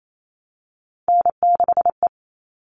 Morse code, heard as a keyed tone, about 28 wpm, 700 Hz. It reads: N6E
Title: All Cut Numbers - 3-Digit 28wpm